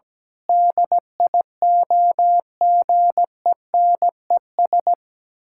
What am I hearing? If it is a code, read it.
Morse code, heard as a keyed tone, about 17 wpm, 700 Hz. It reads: DIOGENES
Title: Pangram Challenge v2 - 17wpm